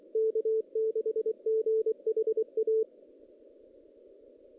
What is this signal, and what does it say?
Morse code, heard as a keyed tone, about 24 wpm, 380 Hz. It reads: K6GHA
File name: 2016 JAN CW NAQP / 2016 Jan naqp k5ot - 0-5